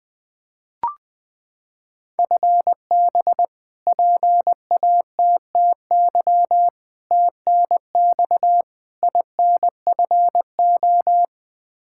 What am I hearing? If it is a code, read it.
Morse code, heard as a keyed tone, about 20 wpm, 700 Hz. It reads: FB PATTY TNX INFO
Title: K1USN SST QSO Practice 20wpm